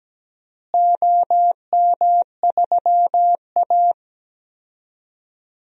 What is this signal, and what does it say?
Morse code, heard as a keyed tone, about 17 wpm, 700 Hz. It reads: OM3A
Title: DXCC-100 Call Signs - 4 Characters - Repeated 3x - 17wpm